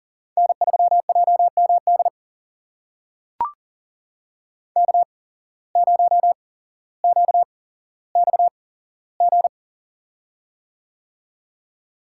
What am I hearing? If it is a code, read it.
Morse code, heard as a keyed tone, about 41 wpm, spaced as for 17 wpm, 700 Hz. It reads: N3JMD K0QXG
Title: USA Call Signs - 5 Characters - 8x Character Spacing - 40wpm